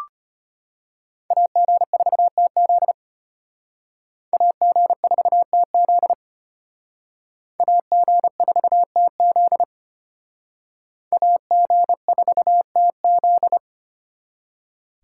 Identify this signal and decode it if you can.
Morse code, heard as a keyed tone, about 32 wpm, 700 Hz. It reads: AG4TZ AG4TZ AG4TZ AG4TZ